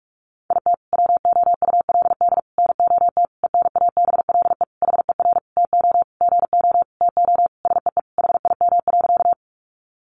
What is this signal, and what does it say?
Morse code, heard as a keyed tone, about 44 wpm, 700 Hz. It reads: IT WOULD NOT ENABLE HER TO GO TO SEE HIM.
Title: Sentences from Top 1500 Words 45wpm